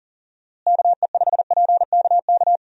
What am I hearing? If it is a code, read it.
Morse code, heard as a keyed tone, about 37 wpm, 700 Hz. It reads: KE5PKK
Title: USA Call Signs - 6 Characters - Repeated 2x - 40wpm